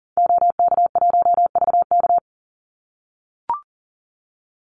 Morse code, heard as a keyed tone, about 40 wpm, 700 Hz. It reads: OK1VK
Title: Call Signs - 40wpm